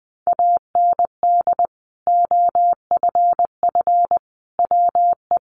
Morse code, heard as a keyed tone, about 20 wpm, 700 Hz. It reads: AND OFF WE
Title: Sentences from Top 200 Words 20wpm